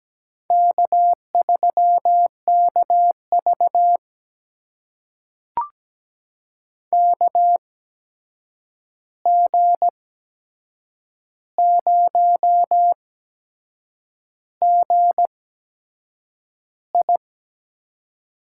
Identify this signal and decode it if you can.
Morse code, heard as a keyed tone, about 17 wpm, spaced as for 7 wpm, 700 Hz. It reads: K3KV KG0GI
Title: USA Call Signs - 8x Character Spacing - 17wpm